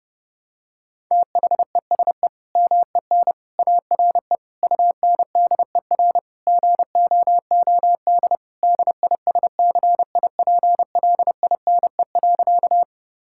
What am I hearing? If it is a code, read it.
Morse code, heard as a keyed tone, about 32 wpm, 700 Hz. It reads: THESE MEN ARE UNDER GOOD DISCIPLINE.
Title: Sentences from Top 1000 Words - Review Part 6 - 30wpm